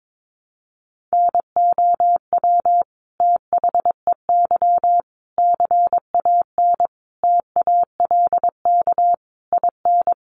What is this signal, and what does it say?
Morse code, heard as a keyed tone, about 22 wpm, 700 Hz. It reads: NOW THEY CAN TALK IN
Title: Sentences from Top 700 Words 22wpm